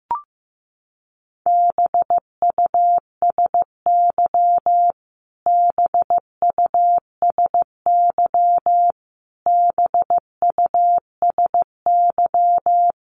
Morse code, heard as a keyed tone, about 15 wpm, 700 Hz. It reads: BUSY BUSY BUSY
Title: Top 500 Words Repeated 15wpm